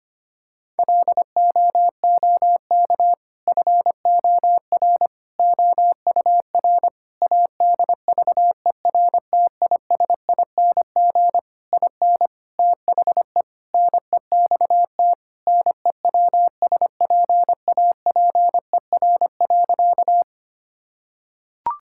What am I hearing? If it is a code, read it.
Morse code, heard as a keyed tone, about 25 wpm, 700 Hz. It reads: LOOK FOR OUR ADVERTISING IN THE NEXT NEWSPAPER.
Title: Sentences from Top 1000 Words - Review Part 6 - 25wpm